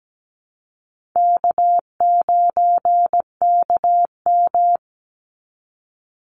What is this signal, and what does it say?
Morse code, heard as a keyed tone, about 17 wpm, 700 Hz. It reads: K9KM
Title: Easy Call Signs Repeated 2x - 17wpm